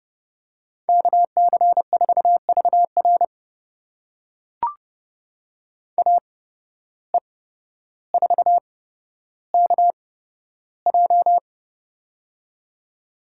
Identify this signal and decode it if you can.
Morse code, heard as a keyed tone, about 29 wpm, spaced as for 13 wpm, 700 Hz. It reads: KC4VR AE4KJ
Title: USA Call Signs - 5 Characters - 8x Character Spacing - 30wpm